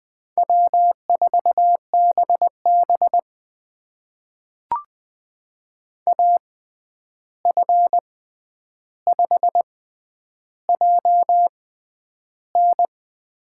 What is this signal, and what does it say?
Morse code, heard as a keyed tone, about 20 wpm, spaced as for 10 wpm, 700 Hz. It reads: W4BB AF5JN
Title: USA Call Signs - 6x Character Spacing - 20wpm